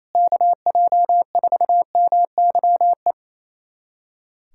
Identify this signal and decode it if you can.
Morse code, heard as a keyed tone, about 27 wpm, 700 Hz. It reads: KJ4MYE
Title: USA Callsign DE Callsign - 6 Characters - 28wpm